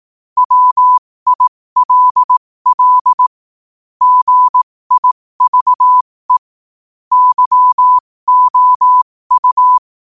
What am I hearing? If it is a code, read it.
Morse code, heard as a keyed tone, about 18 wpm, spaced as for 15 wpm, 1000 Hz. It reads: WILL GIVE YOU
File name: 15wpm